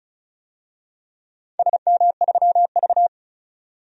Morse code, heard as a keyed tone, about 35 wpm, 700 Hz. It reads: SM3V
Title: DXCC-100 Call Signs - 4 Characters - 35wpm